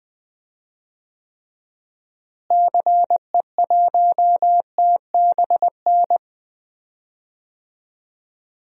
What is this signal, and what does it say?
Morse code, heard as a keyed tone, about 20 wpm, 700 Hz. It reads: CE1TBN
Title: DXCC-100 Call Signs - 6 Characters - Repeated 3x - 20wpm